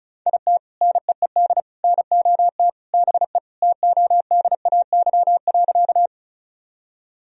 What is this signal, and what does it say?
Morse code, heard as a keyed tone, about 36 wpm, 700 Hz. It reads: IT NEED NOT BE TODAY.
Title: Sentences from Top 1000 Words - Review Part 4 - 35wpm